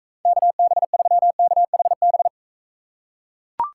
Morse code, heard as a keyed tone, about 43 wpm, 700 Hz. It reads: KB3KHB